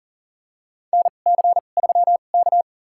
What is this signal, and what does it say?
Morse code, heard as a keyed tone, about 40 wpm, spaced as for 29 wpm, 700 Hz. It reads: NC3K
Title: USA Call Signs - 4 Characters - 2x Character Spacing - 40wpm